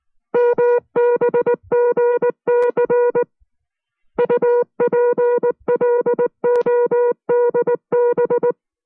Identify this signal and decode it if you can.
Morse code, heard as a keyed tone, about 19 wpm, spaced as for 16 wpm, 480 Hz. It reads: MBGCU PLODB
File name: FreeFiles / CW-practice / 15WPM-side1